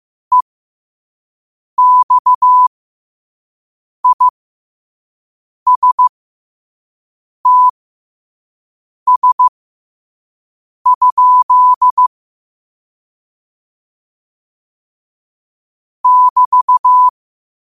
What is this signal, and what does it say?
Morse code, heard as a keyed tone, about 15 wpm, spaced as for 5 wpm, 1000 Hz. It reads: EXISTS? <BT>
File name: qotdfiles / 5wpm